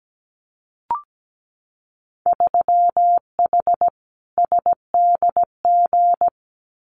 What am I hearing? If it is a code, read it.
Morse code, heard as a keyed tone, about 17 wpm, 700 Hz. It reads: 3H SDG
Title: Winter Field Day Exchanges Encore 17wpm